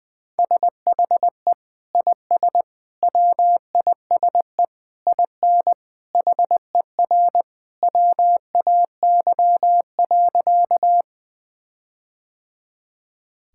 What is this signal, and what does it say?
Morse code, heard as a keyed tone, about 21 wpm, 700 Hz. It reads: SHE IS WISE IN HER WAY.
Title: Sentences from Top 1600 Words 20wpm